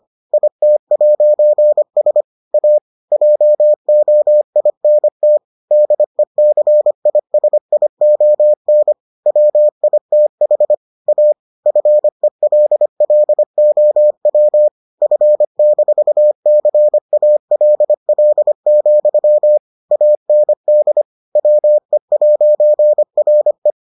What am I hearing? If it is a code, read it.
Morse code, heard as a keyed tone, about 25 wpm, 600 Hz. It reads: IT'S A JOINT DECISION WITH A FELLOW F-CALL, AND WE'RE